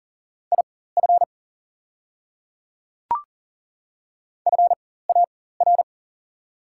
Morse code, heard as a keyed tone, about 39 wpm, spaced as for 18 wpm, 700 Hz. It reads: IF FAR
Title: 4x Character Spacing - Top 300 Words - 40wpm